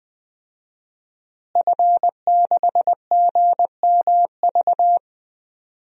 Morse code, heard as a keyed tone, about 21 wpm, 700 Hz. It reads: F6GMV
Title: DXCC-100 Call Signs - 5 Characters - Repeated 3x - 20wpm